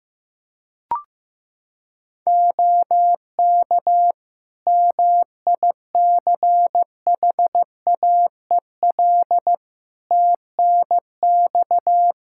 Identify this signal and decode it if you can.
Morse code, heard as a keyed tone, about 16 wpm, 700 Hz. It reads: OK MICHAEL TNX
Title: Practice QSO - Exchange 4 - Ending - 15wpm